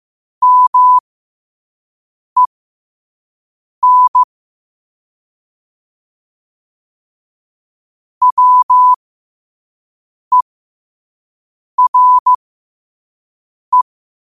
Morse code, heard as a keyed tone, about 15 wpm, spaced as for 5 wpm, 1000 Hz. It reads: MEN WERE
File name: qotdfiles / 5wpm